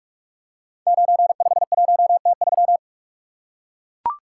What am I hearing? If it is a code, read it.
Morse code, heard as a keyed tone, about 44 wpm, 700 Hz. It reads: 951T3
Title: Code Groups - 5 Letter:Numbers - 45wpm